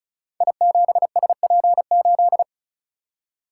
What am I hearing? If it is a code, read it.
Morse code, heard as a keyed tone, about 36 wpm, 690 Hz. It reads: I7SP8
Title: Code Groups - 5 Letter:Numbers - 35wpm